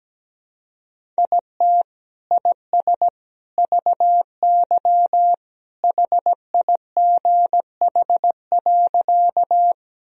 Morse code, heard as a keyed tone, about 17 wpm, 700 Hz. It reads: IT IS VY HIGH.
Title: Sentences from Top 200 Words and Abbreviations 17wpm